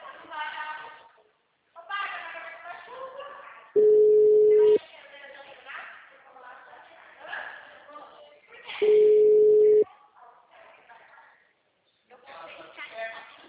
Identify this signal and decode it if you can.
Ringback tone (Europe, 425 Hz), with speech.
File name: out- -2014- - - .86626